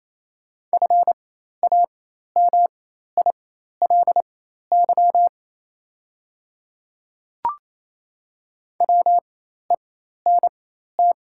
Morse code, heard as a keyed tone, about 28 wpm, spaced as for 13 wpm, 700 Hz. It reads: FAMILY WENT
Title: 4x Character Spacing - Top 500 Words - 28wpm